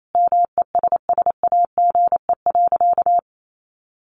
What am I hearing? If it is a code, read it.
Morse code, heard as a keyed tone, about 28 wpm, 700 Hz. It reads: MESSAGE.
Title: Sentences from Top 1100 Words 28wpm